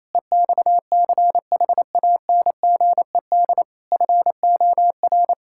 Morse code, heard as a keyed tone, about 28 wpm, 700 Hz. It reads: EXCHANGED FOR
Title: Pangram Challenge v2 - 28wpm